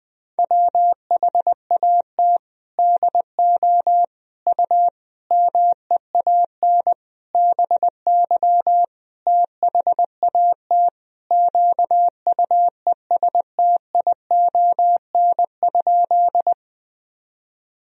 Morse code, heard as a keyed tone, about 20 wpm, 700 Hz. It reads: WHAT DO U MEAN BY THAT QUESTION?
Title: Sentences from Top 400 Words and Abbreviations 20wpm